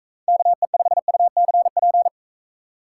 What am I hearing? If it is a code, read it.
Morse code, heard as a keyed tone, about 42 wpm, 700 Hz. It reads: KE5UCP